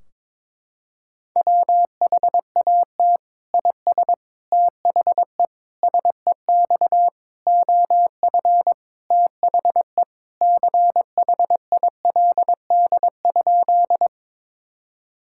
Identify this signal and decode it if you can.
Morse code, heard as a keyed tone, about 22 wpm, 700 Hz. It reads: WHAT IS THE SEX OF THE CHILD?
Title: Sentences from Top 1600 Words 22wpm